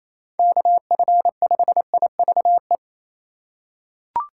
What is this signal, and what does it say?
Morse code, heard as a keyed tone, about 28 wpm, 700 Hz. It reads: KF5IVE
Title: USA Call Signs - 6 Characters - 3x Character Spacing - 28wpm